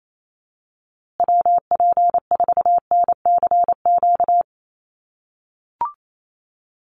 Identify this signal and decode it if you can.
Morse code, heard as a keyed tone, about 28 wpm, 700 Hz. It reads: WP4NCQ
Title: USA Call Signs Repeated 2x 28wpm